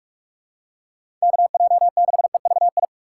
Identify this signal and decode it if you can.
Morse code, heard as a keyed tone, about 46 wpm, 700 Hz. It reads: KJ6EVI
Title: USA Call Signs - 6 Characters - 4x Character Spacing - 45wpm